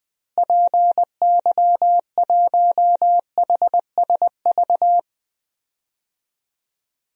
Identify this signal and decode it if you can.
Morse code, heard as a keyed tone, about 20 wpm, 700 Hz. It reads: PY1HSV
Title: DXCC-100 Call Signs - 6 Characters - Repeated 2x - 20wpm